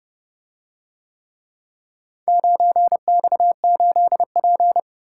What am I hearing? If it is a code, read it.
Morse code, heard as a keyed tone, about 28 wpm, 700 Hz. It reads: 9X8P